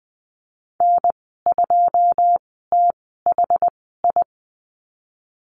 Morse code, heard as a keyed tone, about 19 wpm, spaced as for 14 wpm, 700 Hz. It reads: N2THI
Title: 088 Lesson 15g - Number 5 - Call Signs 20wpm